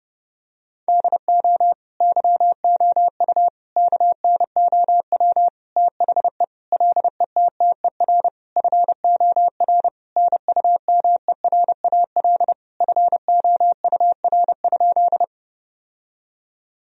Morse code, heard as a keyed tone, about 30 wpm, 700 Hz. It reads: DO YOU KNOW THE LETTER FOR NUMERAL FOUR?
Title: Sentences from Top 1000 Words - Review Part 2 - 30wpm